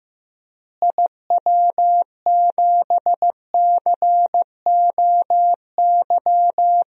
Morse code, heard as a keyed tone, about 15 wpm, 700 Hz. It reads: IW7COY
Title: DXCC-100 Call Signs - 6 Characters - Repeated 2x - 15wpm